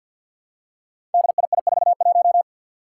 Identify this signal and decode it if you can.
Morse code, heard as a keyed tone, about 48 wpm, 700 Hz. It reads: DII41
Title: Five Challenging Letter-Number Combinations 50wpm